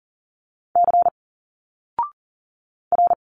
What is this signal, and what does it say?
Morse code, heard as a keyed tone, about 40 wpm, 700 Hz. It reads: C R
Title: Single Letters 40wpm